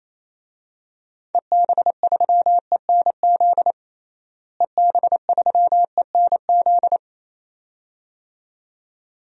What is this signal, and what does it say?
Morse code, heard as a keyed tone, about 27 wpm, 700 Hz. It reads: EB3ENZ EB3ENZ
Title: DXCC-100 Call Signs - 6 Characters - Repeated 2x - 28wpm